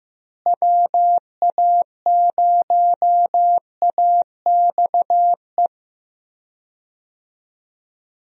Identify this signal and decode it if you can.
Morse code, heard as a keyed tone, about 15 wpm, 700 Hz. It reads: WA0AXE
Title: USA Call Signs - 6 Characters - Repeated 2x - 15wpm